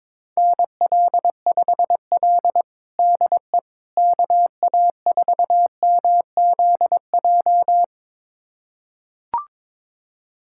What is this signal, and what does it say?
Morse code, heard as a keyed tone, about 22 wpm, 700 Hz. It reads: NL5L DE KA4MZJ
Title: USA Callsign DE Callsign 22wpm